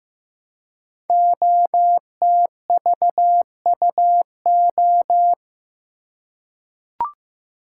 Morse code, heard as a keyed tone, about 15 wpm, 700 Hz. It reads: OTVUO
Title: Code Groups - 5 Letter:Numbers - 15wpm